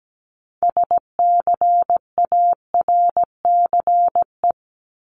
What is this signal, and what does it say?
Morse code, heard as a keyed tone, about 17 wpm, 700 Hz. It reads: SCARCE
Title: LY Suffix - Top 100 Words with Base Words 17wpm